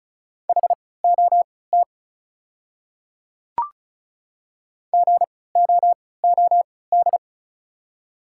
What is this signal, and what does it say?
Morse code, heard as a keyed tone, about 34 wpm, spaced as for 20 wpm, 690 Hz. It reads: HOT GOOD
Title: 3x Character Spacing - Top 500 Words - 35wpm